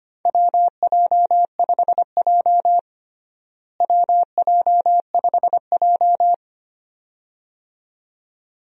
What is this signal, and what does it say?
Morse code, heard as a keyed tone, about 25 wpm, 700 Hz. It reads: WJ5J WJ5J
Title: USA Call Signs - 4 Characters - Repeated 2x - 25wpm